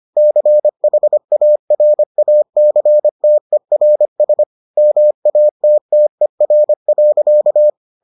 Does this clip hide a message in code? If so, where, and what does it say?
Yes: Morse code from 0.2 to 7.7 s, heard as a keyed tone, about 25 wpm, 600 Hz. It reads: CHARACTERS MATTER.